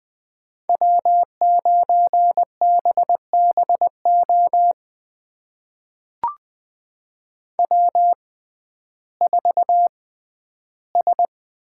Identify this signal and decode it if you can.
Morse code, heard as a keyed tone, about 21 wpm, spaced as for 12 wpm, 700 Hz. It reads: W9BBO W4S
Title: USA Call Signs - 5 Characters - 6x Character Spacing - 20wpm